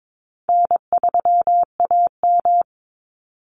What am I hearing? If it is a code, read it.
Morse code, heard as a keyed tone, about 22 wpm, 700 Hz. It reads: N3AM
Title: Easy Call Signs Repeated 3x - 22wpm